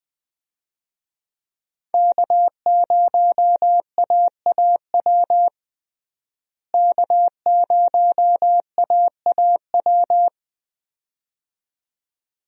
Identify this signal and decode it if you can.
Morse code, heard as a keyed tone, about 20 wpm, 700 Hz. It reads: K0AAW K0AAW
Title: USA Call Signs - 5 Characters - Repeated 2x - 20wpm